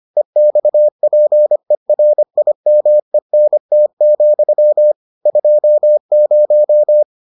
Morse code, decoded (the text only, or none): EXPERIMENT, 20